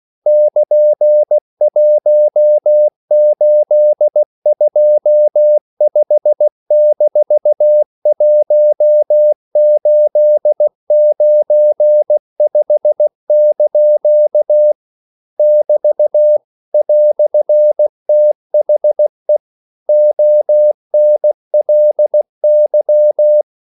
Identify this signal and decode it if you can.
Morse code, heard as a keyed tone, about 16 wpm, 600 Hz. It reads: <KN>1825-1895) <BT> "THE ONLY
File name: CW / Practice / EnglishQuotations0116